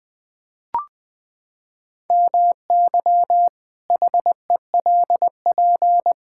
Morse code, heard as a keyed tone, about 20 wpm, 700 Hz. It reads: MY HELP
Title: Pangram Challenge v2 - 20wpm